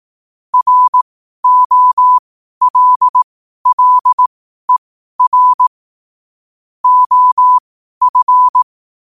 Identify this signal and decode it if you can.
Morse code, heard as a keyed tone, about 18 wpm, spaced as for 12 wpm, 1000 Hz. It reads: ROLLER OF
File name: qotdfiles / 12wpm